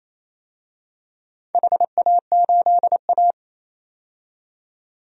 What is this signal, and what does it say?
Morse code, heard as a keyed tone, about 28 wpm, 700 Hz. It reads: HA8A
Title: DXCC-100 Call Signs - 4 Characters - 28wpm